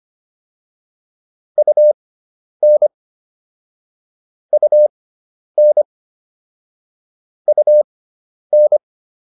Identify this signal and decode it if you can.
Morse code, heard as a keyed tone, about 25 wpm, spaced as for 10 wpm, 600 Hz. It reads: UN UN UN